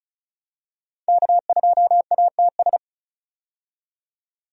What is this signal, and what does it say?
Morse code, heard as a keyed tone, about 36 wpm, 700 Hz. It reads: K2ATS
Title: USA Call Signs - 5 Characters - 35wpm